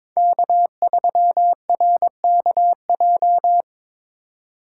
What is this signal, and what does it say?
Morse code, heard as a keyed tone, about 22 wpm, 700 Hz. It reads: K3RKJ
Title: USA Call Signs - 5 Characters - 2x Character Spacing - 22wpm